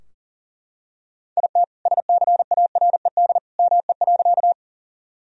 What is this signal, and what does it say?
Morse code, heard as a keyed tone, about 40 wpm, 700 Hz. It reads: IT SCARED ME.
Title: Sentences from Top 1000 Words - Review Part 4 - 40wpm